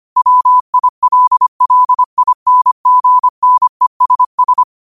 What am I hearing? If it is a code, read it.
Morse code, heard as a keyed tone, about 25 wpm, 1000 Hz. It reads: WILLINGNESS